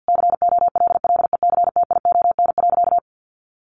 Morse code, heard as a keyed tone, about 50 wpm, 700 Hz. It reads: COLLECTION.
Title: Sentences from Top 1200 Words 50wpm